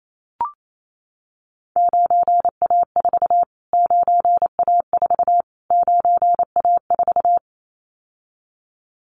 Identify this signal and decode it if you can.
Morse code, heard as a keyed tone, about 28 wpm, 700 Hz. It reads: 9A4 9A4 9A4
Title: Callsign Prefixes for Top 25 DXCC - Repeated 3x - 28wpm